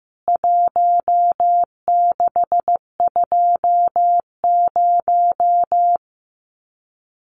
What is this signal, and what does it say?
Morse code, heard as a keyed tone, about 15 wpm, 700 Hz. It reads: 1620
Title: CWT Exchange 15wpm